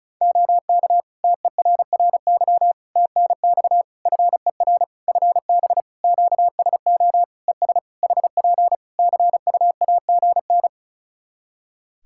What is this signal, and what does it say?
Morse code, heard as a keyed tone, about 36 wpm, 700 Hz. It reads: OK TERRY TNX FER FB QSO ES HP CUAGN
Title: Practice QSO - Exchange 4 - Ending - 35wpm